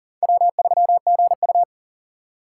Morse code, heard as a keyed tone, about 39 wpm, 690 Hz. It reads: W3GU
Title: USA Call Signs - 4 Characters - 2x Character Spacing - 40wpm